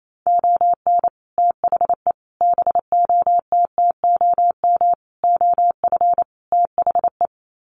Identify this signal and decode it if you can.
Morse code, heard as a keyed tone, about 28 wpm, 700 Hz. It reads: ON THE BOTTOM OF THE
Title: Sets of 5 Words Encore2 28wpm